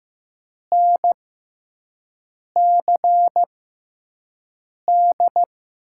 Morse code, heard as a keyed tone, about 15 wpm, spaced as for 5 wpm, 700 Hz. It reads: NCD
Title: USA Call Signs - 6 Characters - 6x Character Spacing - 15wpm